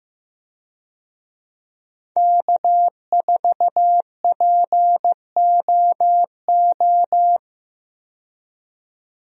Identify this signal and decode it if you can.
Morse code, heard as a keyed tone, about 15 wpm, 700 Hz. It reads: K4POO
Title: USA Call Signs - 5 Characters - 15wpm